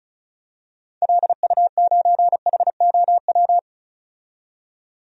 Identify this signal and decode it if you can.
Morse code, heard as a keyed tone, about 34 wpm, 700 Hz. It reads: LU9HOW
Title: DXCC-100 Call Signs - 6 Characters - 35wpm